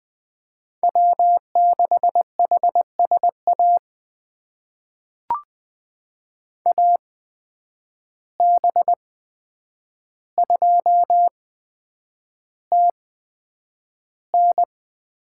Morse code, heard as a keyed tone, about 20 wpm, spaced as for 9 wpm, 700 Hz. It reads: W6HSA AB2TN
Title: USA Call Signs - 8x Character Spacing - 20wpm